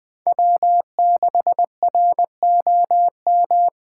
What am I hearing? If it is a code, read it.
Morse code, heard as a keyed tone, about 23 wpm, 700 Hz. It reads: W6ROM
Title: Speed Racing - USA 1x3 Call Signs 20wpm